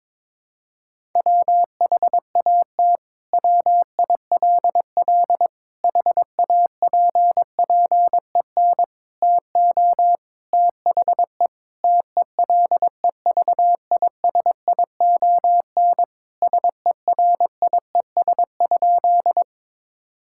Morse code, heard as a keyed tone, about 22 wpm, 700 Hz. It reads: WHAT WILL HAPPEN TO THE TELEVISION SERIES?
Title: Sentences from Top 1000 Words - Review Part 5 - 22wpm